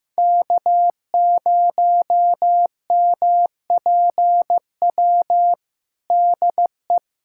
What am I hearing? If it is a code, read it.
Morse code, heard as a keyed tone, about 15 wpm, 700 Hz. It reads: K0MPW DE
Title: USA Callsign DE Callsign 15wpm